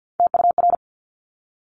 Morse code, heard as a keyed tone, about 50 wpm, 700 Hz. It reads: TUR
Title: Top Three Letter Combinations in Words 50wpm